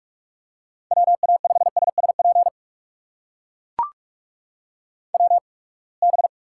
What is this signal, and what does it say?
Morse code, heard as a keyed tone, about 42 wpm, spaced as for 30 wpm, 700 Hz. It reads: WA5SSP WB
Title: USA Call Signs - 6 Characters - 8x Character Spacing - 45wpm